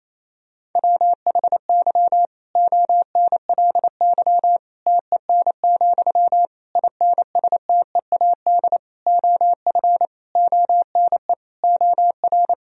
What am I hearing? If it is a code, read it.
Morse code, heard as a keyed tone, about 28 wpm, 700 Hz. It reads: WHY ONLY TEN, INSTEAD OF ONE OR
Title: Sentences from Top 1000 Words - Review Part 4 - 28wpm